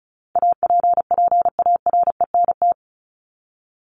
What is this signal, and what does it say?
Morse code, heard as a keyed tone, about 34 wpm, 700 Hz. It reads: APPARENT
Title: LY Suffix - Top 100 Words with Base Words 35wpm